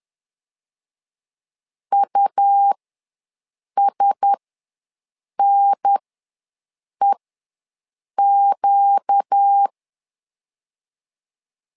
Morse code, heard as a keyed tone, about 10 wpm, spaced as for 6 wpm, 800 Hz. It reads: USNEQ
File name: LEZIONI / 4 Q1S3 ws 10wpm 2437